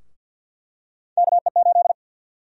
Morse code, heard as a keyed tone, about 49 wpm, 700 Hz. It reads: KE8
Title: USA Call Sign Prefixes - Repeated 2x - 50wpm